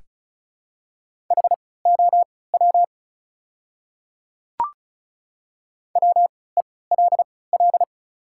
Morse code, heard as a keyed tone, about 36 wpm, spaced as for 20 wpm, 700 Hz. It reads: HOW WELL
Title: 3x Character Spacing - Top 200 Words - 35wpm